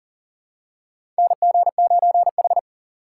Morse code, heard as a keyed tone, about 40 wpm, 700 Hz. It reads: NG9H
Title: USA Call Signs - 4 Characters - 3x Character Spacing - 40wpm